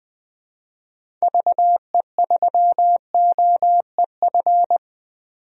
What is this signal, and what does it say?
Morse code, heard as a keyed tone, about 21 wpm, 700 Hz. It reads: VE3OEF
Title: DXCC-100 Call Signs - 6 Characters - 20wpm